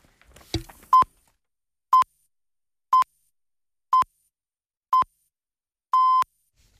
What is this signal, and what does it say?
Radio time signal (time pips). It is heard from 0.9 to 6.2 s.